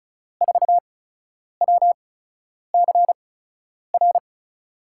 Morse code, heard as a keyed tone, about 36 wpm, spaced as for 10 wpm, 700 Hz. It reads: 4WCR
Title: USA Call Signs - 5 Characters - 8x Character Spacing - 35wpm